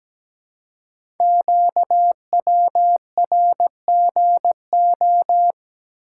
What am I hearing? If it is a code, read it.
Morse code, heard as a keyed tone, about 17 wpm, 700 Hz. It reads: QWRGO
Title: Code Groups - 5 Letters - 17wpm